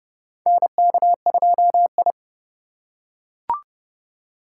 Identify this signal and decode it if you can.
Morse code, heard as a keyed tone, about 31 wpm, 700 Hz. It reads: NK2I
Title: USA Call Signs - 4 Characters - 4x Character Spacing - 30wpm